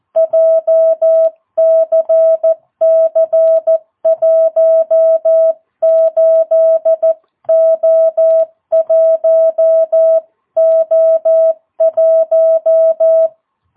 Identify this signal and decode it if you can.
Morse code, heard as a keyed tone, about 14 wpm, 660 Hz. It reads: JCC18O1O1
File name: jh8cbh / 2022cw